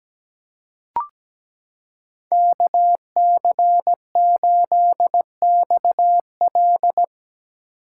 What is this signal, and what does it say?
Morse code, heard as a keyed tone, about 17 wpm, 700 Hz. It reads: KC8XL
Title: Code Groups - 5 Letter:Numbers - 17wpm